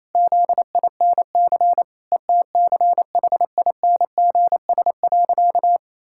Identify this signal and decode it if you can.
Morse code, heard as a keyed tone, about 28 wpm, 700 Hz. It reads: ZINC ETCHINGS.
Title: Pangram Challenge v2 - 28wpm